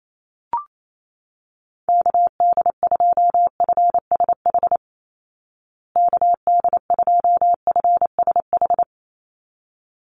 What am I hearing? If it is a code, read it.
Morse code, heard as a keyed tone, about 28 wpm, 690 Hz. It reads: KD2FSH KD2FSH
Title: Call Signs Repeated 3x - 28wpm